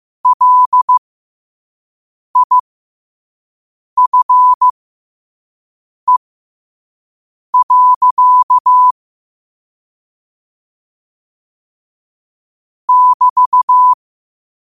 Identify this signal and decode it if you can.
Morse code, heard as a keyed tone, about 15 wpm, spaced as for 5 wpm, 1000 Hz. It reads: LIFE. <BT>
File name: qotdfiles / 5wpm